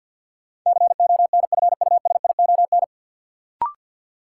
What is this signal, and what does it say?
Morse code, heard as a keyed tone, about 53 wpm, 700 Hz. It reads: CONFUSION
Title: Sentences from Top 1600 Words - New Word Drill - 50wpm